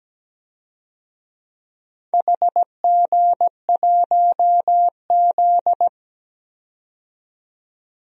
Morse code, heard as a keyed tone, about 17 wpm, 700 Hz. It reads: HG1Z
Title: DXCC-100 Call Signs - 4 Characters - 17wpm